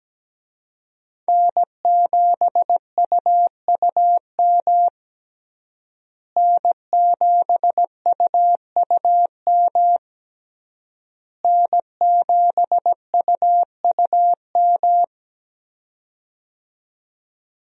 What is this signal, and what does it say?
Morse code, heard as a keyed tone, about 17 wpm, 700 Hz. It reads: N7UUM N7UUM N7UUM
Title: USA Call Signs - 5 Characters - Repeated 3x - 17wpm